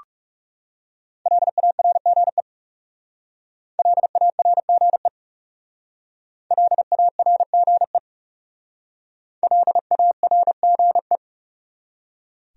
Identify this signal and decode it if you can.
Morse code, heard as a keyed tone, about 40 wpm, 700 Hz. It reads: LARGE LARGE LARGE LARGE